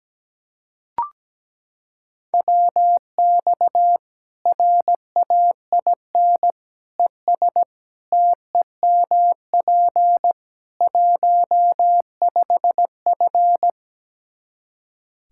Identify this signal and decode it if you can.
Morse code, heard as a keyed tone, about 17 wpm, 700 Hz. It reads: WX RAIN ES TEMP 15F
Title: Practice QSO - Exchange 2 - Info - 17wpm